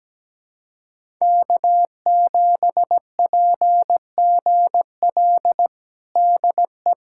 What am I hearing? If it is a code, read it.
Morse code, heard as a keyed tone, about 17 wpm, 700 Hz. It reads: K7PGL DE
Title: USA Callsign DE Callsign - 5 Characters - 17wpm